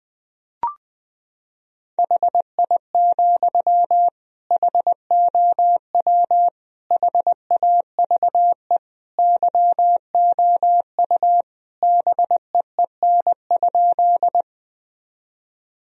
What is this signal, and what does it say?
Morse code, heard as a keyed tone, about 20 wpm, 700 Hz. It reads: HI, HOW HAVE YOU BEEN?
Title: Sentences from Top 1000 Words - Review Part 3 - 20wpm